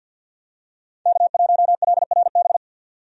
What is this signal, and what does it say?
Morse code, heard as a keyed tone, about 52 wpm, 700 Hz. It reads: K1LRB
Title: USA Call Signs - 5 Characters - 50wpm